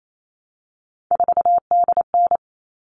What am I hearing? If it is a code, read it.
Morse code, heard as a keyed tone, about 28 wpm, 700 Hz. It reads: 4DN
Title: All Cut Numbers - 3-Digit 28wpm